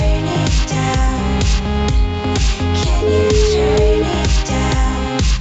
Ringback tone (Europe, 425 Hz), with noise.